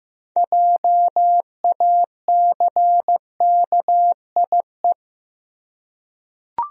Morse code, heard as a keyed tone, about 15 wpm, 700 Hz. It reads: JACKIE
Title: USA Common Names - 15wpm